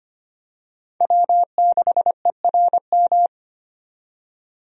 Morse code, heard as a keyed tone, about 25 wpm, 700 Hz. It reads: W6ERM
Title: USA Call Signs - 5 Characters - 25wpm